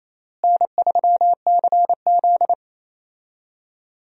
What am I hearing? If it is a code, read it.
Morse code, heard as a keyed tone, about 28 wpm, 700 Hz. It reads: N3CZ
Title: USA Call Signs - 4 Characters - Repeated 2x - 28wpm